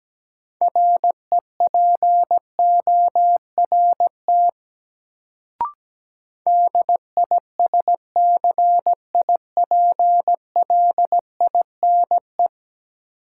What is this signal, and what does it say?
Morse code, heard as a keyed tone, about 17 wpm, 700 Hz. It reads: REPORT DISCIPLINE
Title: Sentences from Top 900 Words - New Word Drill - 17wpm